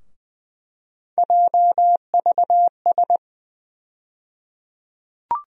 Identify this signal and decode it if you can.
Morse code, heard as a keyed tone, about 20 wpm, 700 Hz. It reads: JVS
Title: USA Call Sign Postfixes - 3 Letters 20wpm